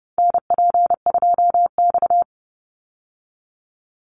Morse code, heard as a keyed tone, about 30 wpm, 700 Hz. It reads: NP2X
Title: Call Signs Repeated 2x - 30wpm